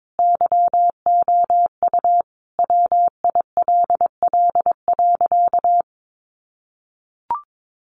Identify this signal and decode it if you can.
Morse code, heard as a keyed tone, about 22 wpm, 700 Hz. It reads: YOU WILL.
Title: Sentences from Top 500 Words 22wpm